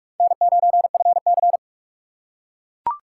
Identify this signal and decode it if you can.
Morse code, heard as a keyed tone, about 42 wpm, 700 Hz. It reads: N9FC